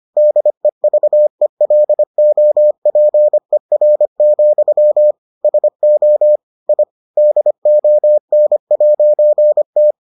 Morse code, heard as a keyed tone, about 25 wpm, 600 Hz. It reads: DEVELOPER, SO I DON'T